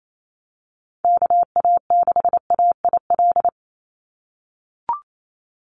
Morse code, heard as a keyed tone, about 28 wpm, 700 Hz. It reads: KA6AIL
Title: Call Signs - 28wpm